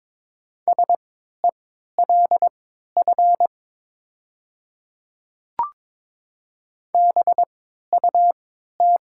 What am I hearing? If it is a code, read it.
Morse code, heard as a keyed tone, about 22 wpm, spaced as for 12 wpm, 700 Hz. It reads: SELF BUT
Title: 3x Character Spacing - Top 300 Words - 22wpm